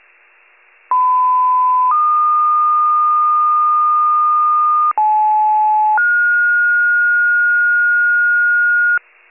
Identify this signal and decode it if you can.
Two-tone page, heard as a one-second tone, then a three-second tone, about 1010 then 1220 Hz, about 860 then 1430 Hz, with noise. It is heard from 0.9 to 9.0 s.
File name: pager / 75 FIRE / wcalert